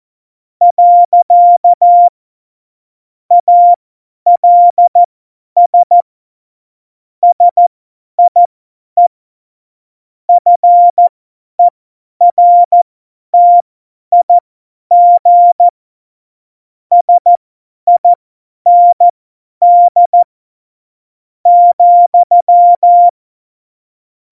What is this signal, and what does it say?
Morse code, heard as a keyed tone, about 14 wpm, spaced as for 9 wpm, 700 Hz. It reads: . ALS SIE FERTIG SIND,
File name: Morsekurs / Witze / Witz 05. Juni 2018, 50 F70